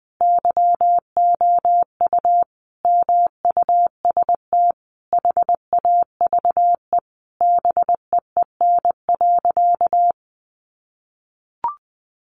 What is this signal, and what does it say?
Morse code, heard as a keyed tone, about 20 wpm, 700 Hz. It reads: YOU MUST HAVE BEEN.
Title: Sentences from Top 1000 Words 20wpm